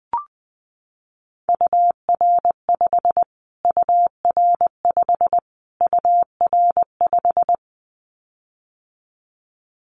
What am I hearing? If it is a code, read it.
Morse code, heard as a keyed tone, about 20 wpm, 700 Hz. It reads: UR5 UR5 UR5
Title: Callsign Prefixes for Top 25 DXCC - Repeated 3x - 20wpm